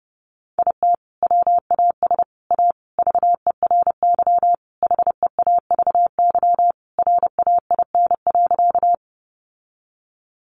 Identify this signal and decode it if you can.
Morse code, heard as a keyed tone, about 30 wpm, 700 Hz. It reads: IT WAS A VERY HEAVY RAIN.
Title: Sentences from Top 1100 Words 30wpm